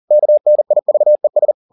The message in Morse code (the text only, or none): KNIVES